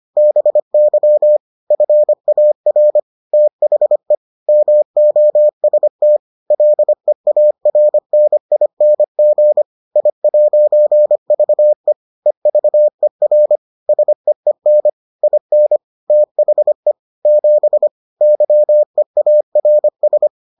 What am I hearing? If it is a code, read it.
Morse code, heard as a keyed tone, about 25 wpm, 600 Hz. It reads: BY FAR THE MOST LEARNING I'VE EVER SEEN IN THE 7 YEARS